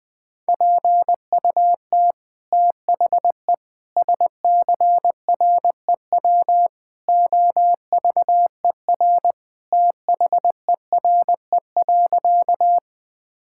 Morse code, heard as a keyed tone, about 21 wpm, 700 Hz. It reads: PUT THE SCREW OVER THERE.
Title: Sentences from Top 1600 Words 20wpm